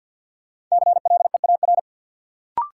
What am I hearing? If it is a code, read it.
Morse code, heard as a keyed tone, about 52 wpm, 700 Hz. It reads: CLEAR
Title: Sentences from Top 500 Words - New Word Drill - 50wpm